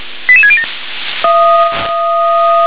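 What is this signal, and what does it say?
Five-tone selective call (ZVEI), heard as quick runs of five tones, tones of about 70 ms. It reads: 88607
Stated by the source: ZVEI five-tone selective call received on SDR.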